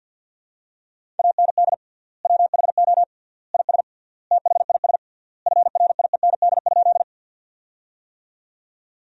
Morse code, heard as a keyed tone, about 50 wpm, 700 Hz. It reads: AND WHO IS THIS FRIEND?
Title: 3x Word-Spacing - Sentences from Top 300 Words - 50wpm